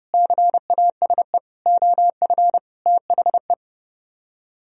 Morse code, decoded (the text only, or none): CASE OF THE